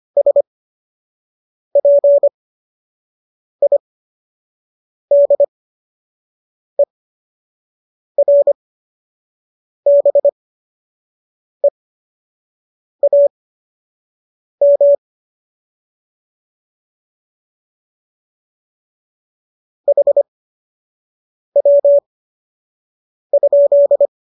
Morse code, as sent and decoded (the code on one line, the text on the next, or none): ... .--. .. -.. .-. -... .- -- / .... .-- ..--..
SPIDERBEAM HW?